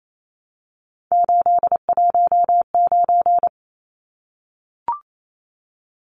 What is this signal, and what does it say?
Morse code, heard as a keyed tone, about 28 wpm, 700 Hz. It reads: 819
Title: Numbers - 3-Digit 28wpm